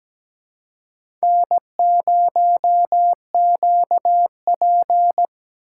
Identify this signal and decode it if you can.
Morse code, heard as a keyed tone, about 17 wpm, 700 Hz. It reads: N0QP
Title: USA Call Signs - 4 Characters - 4x Character Spacing - 17wpm